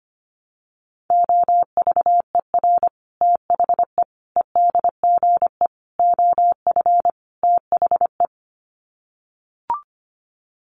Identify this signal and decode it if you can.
Morse code, heard as a keyed tone, about 25 wpm, 700 Hz. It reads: OVER THE EDGE OF THE
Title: Sets of 5 Words Encore2 25wpm